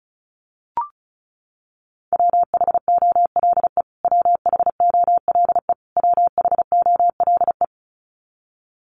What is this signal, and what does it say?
Morse code, heard as a keyed tone, about 34 wpm, 700 Hz. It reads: WHOLE WHOLE WHOLE
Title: Top 500 Words Repeated 35wpm